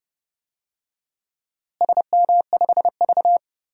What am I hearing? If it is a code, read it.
Morse code, heard as a keyed tone, about 30 wpm, 700 Hz. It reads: SM5V
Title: DXCC-100 Call Signs - 4 Characters - Repeated 2x - 30wpm